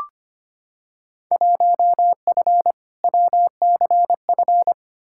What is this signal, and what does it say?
Morse code, heard as a keyed tone, about 25 wpm, 700 Hz. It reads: 1F WCF
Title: Field Day Exchanges - 25wpm